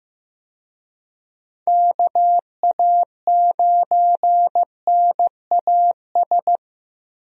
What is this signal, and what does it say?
Morse code, heard as a keyed tone, about 15 wpm, 700 Hz. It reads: KA9NAS
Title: USA Call Signs - 6 Characters - Repeated 2x - 15wpm